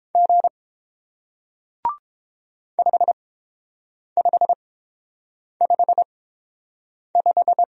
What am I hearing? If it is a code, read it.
Morse code, heard as a keyed tone, about 30 wpm, spaced as for 7 wpm, 700 Hz. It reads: G 5555